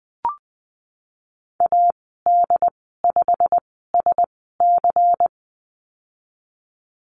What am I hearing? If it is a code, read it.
Morse code, heard as a keyed tone, about 20 wpm, spaced as for 15 wpm, 700 Hz. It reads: AD5SC
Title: 147 Lesson 24g - Letter y - Call Signs 20wpm